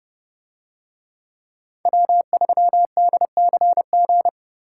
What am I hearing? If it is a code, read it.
Morse code, heard as a keyed tone, about 31 wpm, 700 Hz. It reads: W3DCG